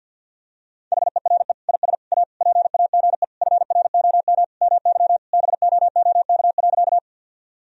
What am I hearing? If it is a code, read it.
Morse code, heard as a keyed tone, about 46 wpm, 700 Hz. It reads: HERE IS A PAGE FROM MY BOOK.
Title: Sentences from Top 1000 Words - Review Part 1 - 50wpm